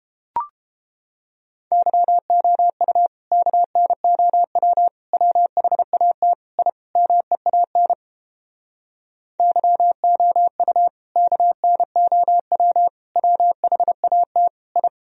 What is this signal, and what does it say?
Morse code, heard as a keyed tone, about 29 wpm, 700 Hz. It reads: YOU KNOW WHAT I MEAN YOU KNOW WHAT I